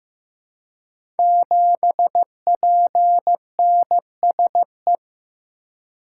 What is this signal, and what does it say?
Morse code, heard as a keyed tone, about 15 wpm, 700 Hz. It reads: 7PNSE
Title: Code Groups - 5 Letter:Numbers - 15wpm